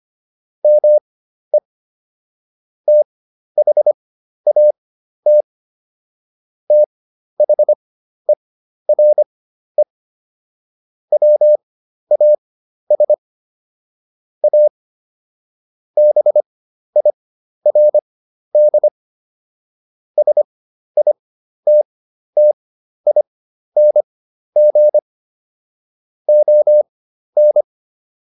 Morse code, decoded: ME THAT THERE WAS A BIRD SITTING ON

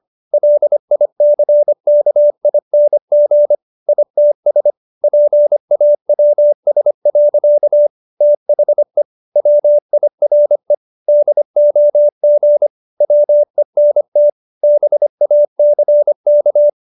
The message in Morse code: LICKING ITS PAWS. THE WIRE DOG WENT BACK

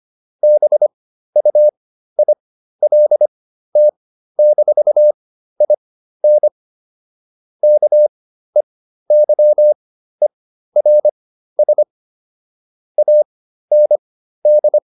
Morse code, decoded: BUILT-IN KEYERS AND

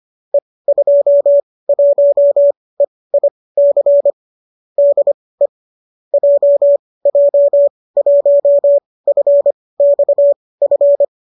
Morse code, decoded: E21EIC DE JJ1FXF

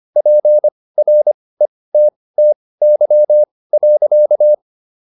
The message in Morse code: PRETTY.

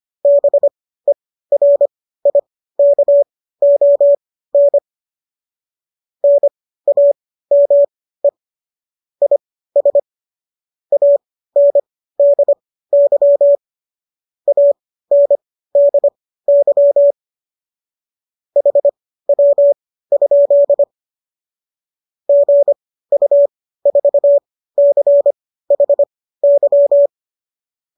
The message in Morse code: BERIKON NAME IS ANDY ANDY HW? GU4CHY

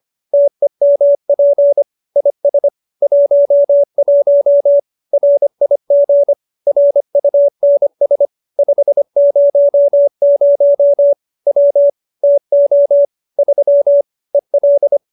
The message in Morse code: TEMP IS 11 RIG RUNS 500 W TO 3 EL

